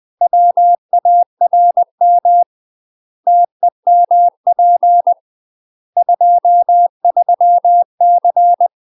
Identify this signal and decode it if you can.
Morse code, heard as a keyed tone, about 20 wpm, 700 Hz. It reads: WARM TEMP 23C
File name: UK MASTER QSOs TRAINING / 20WPM / UK MASTER QSOs 0009